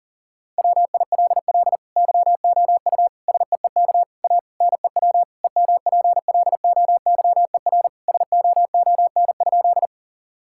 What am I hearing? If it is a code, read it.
Morse code, heard as a keyed tone, about 37 wpm, 700 Hz. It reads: WILL YOU SEEK A NEW EMPLOYER SOON?
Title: Sentences from Top 1600 Words 40wpm